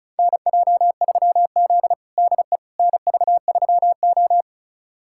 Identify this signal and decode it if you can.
Morse code, heard as a keyed tone, about 34 wpm, 700 Hz. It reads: NJ3Z DE NV3O
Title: USA Callsign DE Callsign - 4 Characters - 35wpm